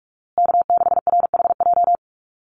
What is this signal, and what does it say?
Morse code, heard as a keyed tone, about 44 wpm, 700 Hz. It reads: K6RHJ
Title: USA Call Signs Repeated 3x 45wpm